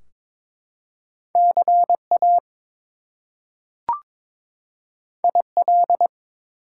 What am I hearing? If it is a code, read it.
Morse code, heard as a keyed tone, about 22 wpm, 700 Hz. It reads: CA IL
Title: US States and Canadian Provinces 22wpm